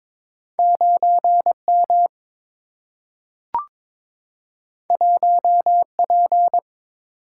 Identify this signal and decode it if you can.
Morse code, heard as a keyed tone, about 22 wpm, 700 Hz. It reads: 9 M 1 P
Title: Single-Letter Single-Number 22wpm